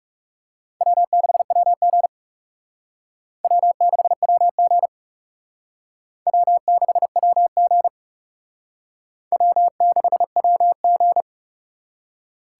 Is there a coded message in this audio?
Yes: Morse code from 0.8 to 11.2 s, heard as a keyed tone, about 38 wpm, 700 Hz. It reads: W6WG W6WG W6WG W6WG